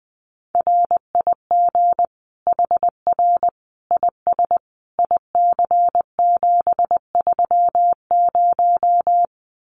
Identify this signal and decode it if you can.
Morse code, heard as a keyed tone, about 19 wpm, 700 Hz. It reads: RIG HR IS IC730
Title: Common QSO Phrases 20wpm